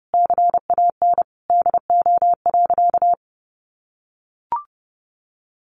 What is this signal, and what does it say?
Morse code, heard as a keyed tone, about 30 wpm, 700 Hz. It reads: CAN DO.
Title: Sentences from Top 900 Words 30wpm